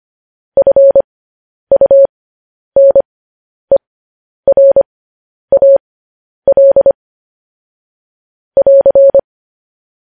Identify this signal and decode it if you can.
Morse code, heard as a keyed tone, about 25 wpm, spaced as for 10 wpm, 550 Hz. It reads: FUNERAL <AR>
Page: news headlines in morse code at 10 wpm